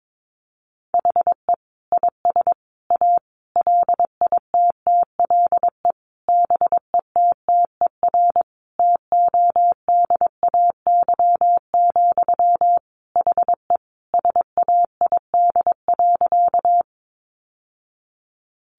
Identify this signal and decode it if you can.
Morse code, heard as a keyed tone, about 22 wpm, 700 Hz. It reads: HE IS A LITTLE BETTER TODAY, HE SAID.
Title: Sentences from Top 600 Words 22wpm